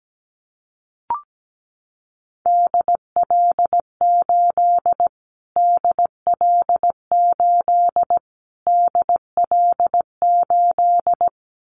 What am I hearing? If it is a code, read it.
Morse code, heard as a keyed tone, about 17 wpm, 700 Hz. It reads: DL8 DL8 DL8
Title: Callsign Prefixes for Top 25 DXCC - Repeated 3x - 17wpm